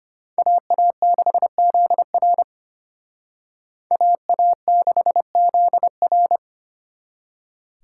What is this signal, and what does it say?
Morse code, heard as a keyed tone, about 30 wpm, 700 Hz. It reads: AA6ZR AA6ZR
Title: Speed Racing - USA Call Signs 25wpm